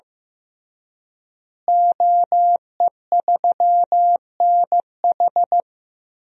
Morse code, heard as a keyed tone, about 15 wpm, 700 Hz. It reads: OE3NH
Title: DXCC-100 Call Signs - 5 Characters - Repeated 2x - 15wpm